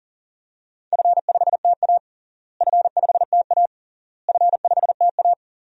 Morse code, heard as a keyed tone, about 40 wpm, 700 Hz. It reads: F5TA F5TA F5TA
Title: DXCC-100 Call Signs - 4 Characters - Repeated 3x - 40wpm